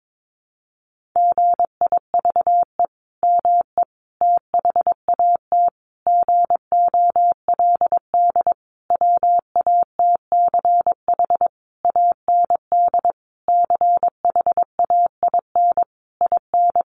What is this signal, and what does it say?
Morse code, heard as a keyed tone, about 22 wpm, 700 Hz. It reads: GIVE ME THAT GOLD WATCH AND CHAIN IN
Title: Sentences from Top 1500 Words 22wpm